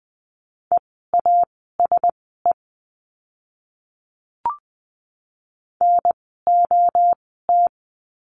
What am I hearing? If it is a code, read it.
Morse code, heard as a keyed tone, about 20 wpm, spaced as for 15 wpm, 700 Hz. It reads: EASE NOT
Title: 054 Lesson 10f - Letter r - Words 20wpm